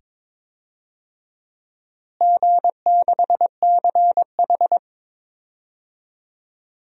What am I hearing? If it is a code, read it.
Morse code, heard as a keyed tone, about 22 wpm, 700 Hz. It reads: G6CH
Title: DXCC-100 Call Signs - 4 Characters - Repeated 2x - 22wpm